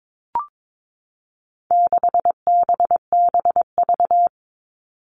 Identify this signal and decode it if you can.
Morse code, heard as a keyed tone, about 22 wpm, 700 Hz. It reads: 6BBV
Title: All Cut Numbers - 4-Digit 22wpm